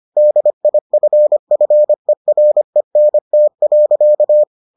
Morse code, heard as a keyed tone, about 25 wpm, 600 Hz. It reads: DIFFERENT.